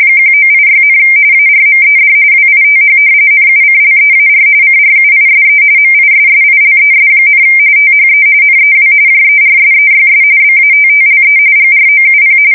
RTTY, about 45.45 baud. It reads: 1986, ARMED WITH NUCLEAR CRUISE MISSILES, THE NEW JERSEYS BATTLE GROUP
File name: fsk170 / battleshipnj